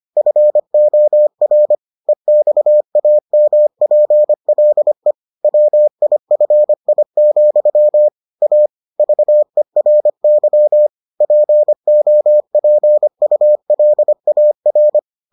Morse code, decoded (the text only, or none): FOR EXAMPLE WIFI, A VERY POPULAR